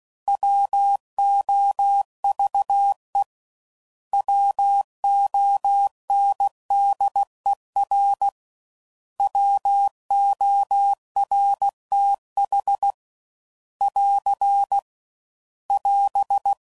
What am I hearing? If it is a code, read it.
Morse code, heard as a keyed tone, about 16 wpm, 800 Hz. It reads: WOVE WONDER WORTH <AR> <AS>
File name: Prefixes / By First Letter / Ws